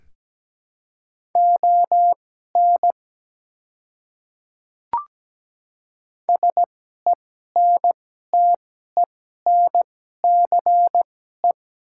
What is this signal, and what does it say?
Morse code, heard as a keyed tone, about 17 wpm, spaced as for 12 wpm, 700 Hz. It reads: ON SENTENCE
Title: 2x Character Spacing - Top 200 Words - 17wpm